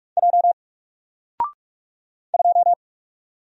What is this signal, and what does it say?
Morse code, heard as a keyed tone, about 44 wpm, 700 Hz. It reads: J 2
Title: Single Letter-Number Alt 45wpm